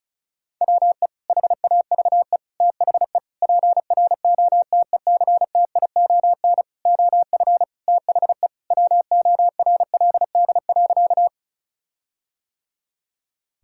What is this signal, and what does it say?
Morse code, heard as a keyed tone, about 34 wpm, 700 Hz. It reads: WE HAVE THE PROTECTION OF THE WORLD.
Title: Sentences from Top 1600 Words 35wpm